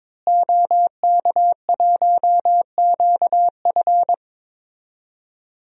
Morse code, heard as a keyed tone, about 22 wpm, 700 Hz. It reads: OK1QF
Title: DXCC-100 Call Signs - 5 Characters - Repeated 2x - 22wpm